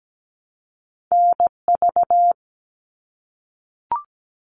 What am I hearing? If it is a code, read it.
Morse code, heard as a keyed tone, about 17 wpm, 700 Hz. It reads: NV
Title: US States Repeated 3x 17wpm